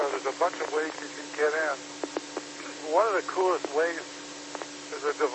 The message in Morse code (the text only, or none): none